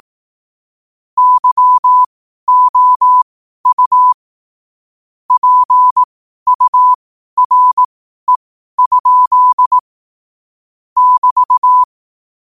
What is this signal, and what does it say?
Morse code, heard as a keyed tone, about 18 wpm, spaced as for 12 wpm, 1000 Hz. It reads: YOU PURE? <BT>
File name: qotdfiles / 12wpm